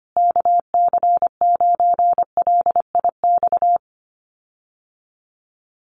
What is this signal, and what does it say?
Morse code, heard as a keyed tone, about 25 wpm, 700 Hz. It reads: KC9LIX
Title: USA Call Signs Repeated 3x 25wpm